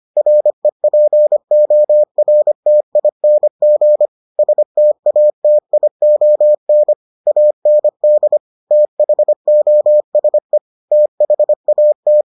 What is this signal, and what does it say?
Morse code, heard as a keyed tone, about 25 wpm, 600 Hz. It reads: REPORTING STATION AND THOSE THAT